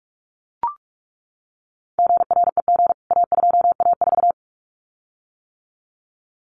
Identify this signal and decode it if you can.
Morse code, heard as a keyed tone, about 44 wpm, 700 Hz. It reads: GREG A2A4
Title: CWT Exchange 45wpm